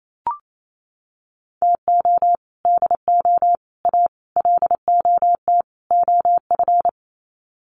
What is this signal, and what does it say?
Morse code, heard as a keyed tone, about 28 wpm, 700 Hz. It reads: TO DO A LOT OF
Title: Sets of 5 Words Encore 28wpm